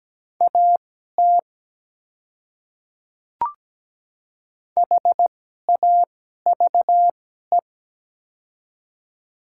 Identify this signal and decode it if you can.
Morse code, heard as a keyed tone, about 17 wpm, spaced as for 12 wpm, 700 Hz. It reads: AT HAVE
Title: 2x Character Spacing - Top 100 Words - 17wpm